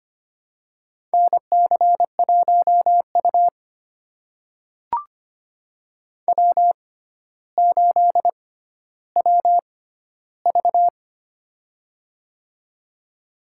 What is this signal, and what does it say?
Morse code, heard as a keyed tone, about 25 wpm, spaced as for 13 wpm, 700 Hz. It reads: NC1U W8WV
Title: USA Call Signs - 4 Characters - 6x Character Spacing - 25wpm